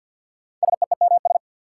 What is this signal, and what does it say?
Morse code, heard as a keyed tone, about 53 wpm, 700 Hz. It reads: SEEMS
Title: Sets of 3 Words - New Word Drill - 50wpm